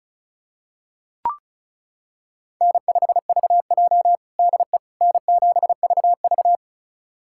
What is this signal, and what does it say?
Morse code, heard as a keyed tone, about 36 wpm, 700 Hz. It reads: N5VJ DE N7VV
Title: USA Callsign DE Callsign - 4 Characters - 35wpm